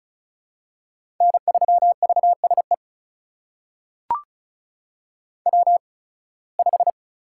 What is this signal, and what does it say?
Morse code, heard as a keyed tone, about 36 wpm, spaced as for 23 wpm, 700 Hz. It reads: N3VSE W5
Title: USA Call Signs - 5 Characters - 8x Character Spacing - 35wpm